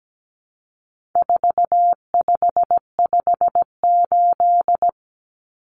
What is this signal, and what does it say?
Morse code, heard as a keyed tone, about 17 wpm, 700 Hz. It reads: 4558
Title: Numbers - 4-Digit 17wpm